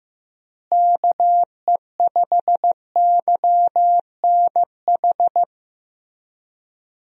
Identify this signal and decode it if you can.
Morse code, heard as a keyed tone, about 15 wpm, 700 Hz. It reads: KE5YNH
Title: USA Call Signs - 6 Characters - Repeated 3x - 15wpm